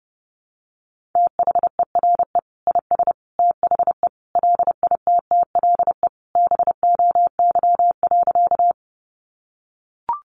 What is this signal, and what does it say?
Morse code, heard as a keyed tone, about 28 wpm, 700 Hz. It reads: THERE IS THE LITTLE BOY.
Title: Sentences from Top 200 Words 30wpm